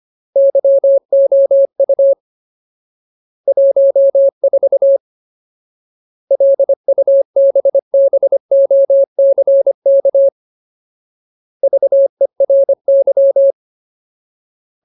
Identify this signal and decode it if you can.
Morse code, heard as a keyed tone, about 25 wpm, 550 Hz. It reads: YOU 14 LUBBOCK VERY
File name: Ultimate Word List / 25WPM Ultimate